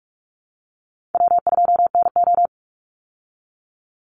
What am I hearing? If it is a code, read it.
Morse code, heard as a keyed tone, about 46 wpm, 700 Hz. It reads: W2NO
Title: Call Signs - 45wpm